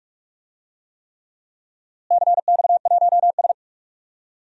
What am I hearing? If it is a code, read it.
Morse code, heard as a keyed tone, about 46 wpm, 700 Hz. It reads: CX1S
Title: DXCC-100 Call Signs - 4 Characters - Repeated 3x - 45wpm